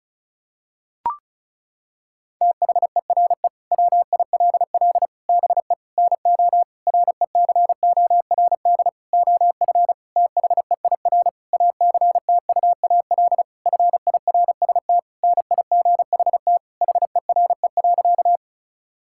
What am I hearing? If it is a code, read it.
Morse code, heard as a keyed tone, about 36 wpm, 700 Hz. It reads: THERE WILL BE NO RECORD OF THEIR ACTUAL FIRST NIGHT HERE.
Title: Sentences from Top 1000 Words - Review Part 5 - 35wpm